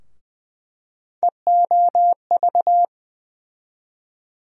Morse code, heard as a keyed tone, about 20 wpm, 700 Hz. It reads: EOV
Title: USA Call Sign Postfixes - 3 Letters 20wpm